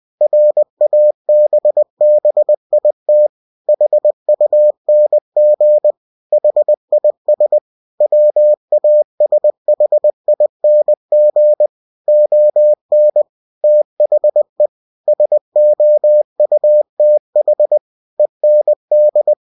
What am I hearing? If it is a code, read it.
Morse code, heard as a keyed tone, about 20 wpm, 600 Hz. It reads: RABBIT HUNG HIS WASHING ON THE SOUTH END
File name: cw-books / winnie-the-pooh / winnie-20wpm-page-0019